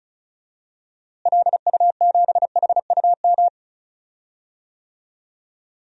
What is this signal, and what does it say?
Morse code, heard as a keyed tone, about 36 wpm, 700 Hz. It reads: LU7HUM
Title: DXCC-100 Call Signs - 6 Characters - Repeated 2x - 35wpm